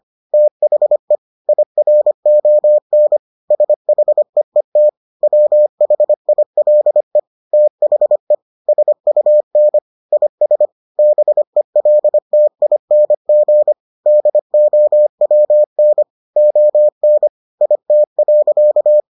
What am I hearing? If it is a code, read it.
Morse code, heard as a keyed tone, about 25 wpm, 600 Hz. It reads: THE IRON SHEET WHILE THE SUN IS BELTING DOWN ON IT.